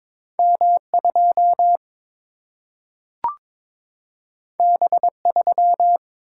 Morse code, heard as a keyed tone, about 22 wpm, 700 Hz. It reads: M 2 B 3
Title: Single-Letter Single-Number 22wpm